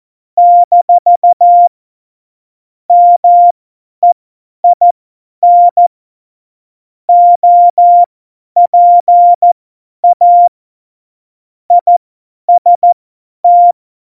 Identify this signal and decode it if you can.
Morse code, heard as a keyed tone, about 14 wpm, spaced as for 10 wpm, 700 Hz. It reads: - MEIN OPA IST
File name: Morsekurs / Witze / Witz 27. Oktober 2020, 50 F70